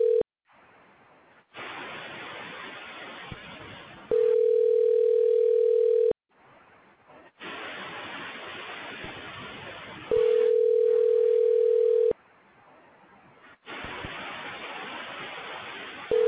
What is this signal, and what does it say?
Ringback tone (North America), 440 + 480 Hz.